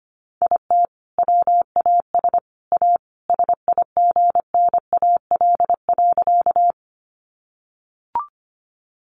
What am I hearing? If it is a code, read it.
Morse code, heard as a keyed tone, about 25 wpm, 700 Hz. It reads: IT WAS A SIGNAL.
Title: Sentences from Top 1400 Words 25wpm